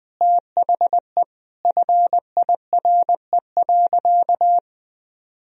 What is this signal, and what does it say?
Morse code, heard as a keyed tone, about 20 wpm, 700 Hz. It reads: THE FIRE.
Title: Sentences from Top 1000 Words - Review Part 1 - 20wpm